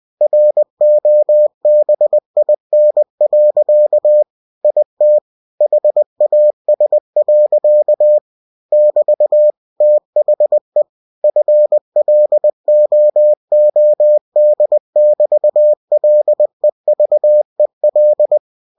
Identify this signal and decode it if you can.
Morse code, heard as a keyed tone, about 20 wpm, 600 Hz. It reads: ROBIN. IT HAS. <BT> THE FLOOD<BT>LEVEL
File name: cw-books / winnie-the-pooh / winnie-20wpm-page-0096